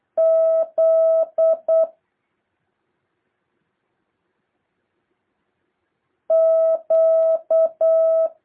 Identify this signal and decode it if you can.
Morse code, heard as a keyed tone, about 8 wpm, 650 Hz. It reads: Z Q
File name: jh8cbh / 2023cw